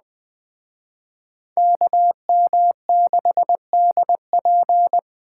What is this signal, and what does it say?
Morse code, heard as a keyed tone, about 20 wpm, 700 Hz. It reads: KM6DP
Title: Speed Racing - USA 2x2 Call Signs 17wpm